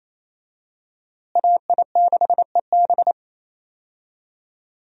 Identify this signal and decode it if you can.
Morse code, heard as a keyed tone, about 28 wpm, 700 Hz. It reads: AI6EB
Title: USA Call Signs - 5 Characters - Repeated 3x - 28wpm